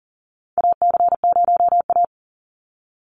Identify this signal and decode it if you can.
Morse code, heard as a keyed tone, about 40 wpm, 700 Hz. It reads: AC0A
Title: NA SOTA Chaser Callsigns 40wpm